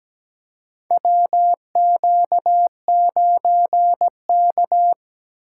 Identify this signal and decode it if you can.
Morse code, heard as a keyed tone, about 17 wpm, 700 Hz. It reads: WQ9K
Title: USA Call Signs - 4 Characters - 8x Character Spacing - 17wpm